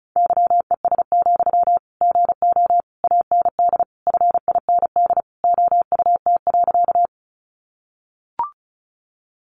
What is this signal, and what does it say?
Morse code, heard as a keyed tone, about 34 wpm, 700 Hz. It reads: YES, GO AND FIND OUT.
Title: Sentences from Top 500 Words 35wpm